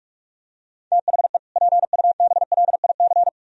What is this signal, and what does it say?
Morse code, heard as a keyed tone, about 45 wpm, 700 Hz. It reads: THE PUBLIC